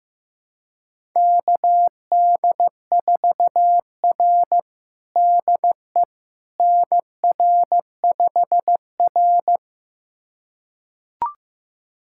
Morse code, heard as a keyed tone, about 15 wpm, 700 Hz. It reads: KD4R DE NR5R
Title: USA Callsign DE Callsign - 4 Characters - 15wpm